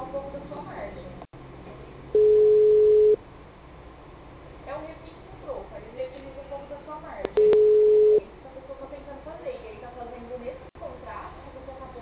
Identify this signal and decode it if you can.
Ringback tone (Europe, 425 Hz), with speech.